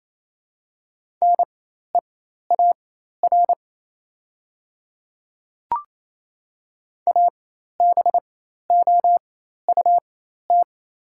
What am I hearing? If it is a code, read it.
Morse code, heard as a keyed tone, about 28 wpm, spaced as for 13 wpm, 700 Hz. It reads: NEAR ABOUT
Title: 4x Character Spacing - Top 200 Words - 28wpm